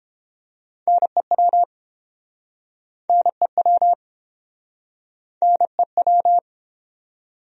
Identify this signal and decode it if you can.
Morse code, heard as a keyed tone, about 31 wpm, 700 Hz. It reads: NEW NEW NEW